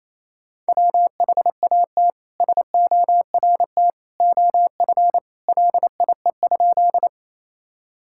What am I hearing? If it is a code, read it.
Morse code, heard as a keyed tone, about 28 wpm, 700 Hz. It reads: WHAT SORT OF LIE?
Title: Sentences from Top 1600 Words 28wpm